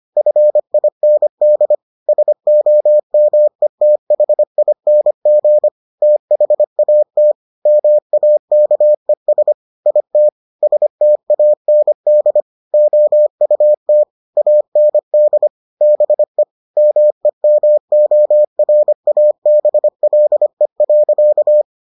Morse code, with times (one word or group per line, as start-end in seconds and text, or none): 0.2-1.8 FIND
2.1-5.7 SOMETHING
6.0-7.3 THAT
7.7-9.5 MAKES
9.9-10.3 IT
10.6-12.4 STAND
12.7-14.0 OUT
14.4-15.5 AND
15.8-16.4 BE
16.8-21.6 MEMORABLE.